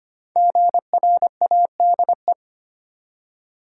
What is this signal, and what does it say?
Morse code, heard as a keyed tone, about 25 wpm, 700 Hz. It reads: GRADE
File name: 5 Letter Words - First 3 Letters Given - 25wpm